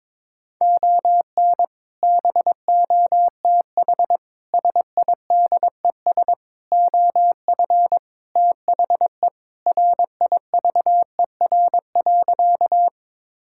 Morse code, heard as a keyed tone, about 22 wpm, 700 Hz. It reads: ON BOTH SIDES OF THE RIVER.
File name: practice / 22wpm / Sentences with 4-Word Phrases - 22wpm